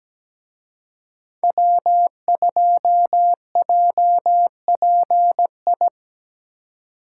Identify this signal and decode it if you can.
Morse code, heard as a keyed tone, about 20 wpm, 700 Hz. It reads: W2JPI
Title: Speed Racing - USA 1x3 Call Signs 17wpm